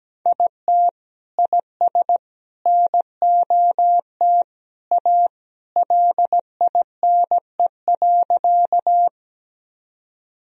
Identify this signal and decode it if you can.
Morse code, heard as a keyed tone, about 17 wpm, 700 Hz. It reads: IT IS NOT A LINE.
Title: Sentences from Top 1000 Words - Review Part 1 - 17wpm